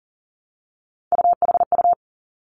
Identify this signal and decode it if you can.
Morse code, heard as a keyed tone, about 38 wpm, 700 Hz. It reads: UHU
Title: Three Challenging Letter-Number Combinations 40wpm